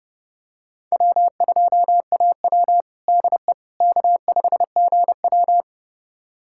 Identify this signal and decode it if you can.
Morse code, heard as a keyed tone, about 32 wpm, 700 Hz. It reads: W2AW DE K5GW
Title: USA Callsign DE Callsign - 4 Characters - 30wpm